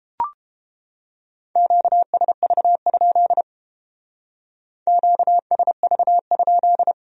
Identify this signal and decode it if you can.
Morse code, heard as a keyed tone, about 29 wpm, 700 Hz. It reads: QSV? QSV?